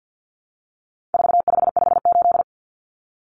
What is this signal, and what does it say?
Morse code, heard as a keyed tone, about 50 wpm, 700 Hz. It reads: 4558
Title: Numbers - 4-Digit 50wpm